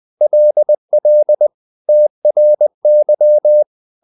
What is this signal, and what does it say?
Morse code, heard as a keyed tone, about 20 wpm, 600 Hz. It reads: LL TRY
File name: the old man and the sea part 2 20wpm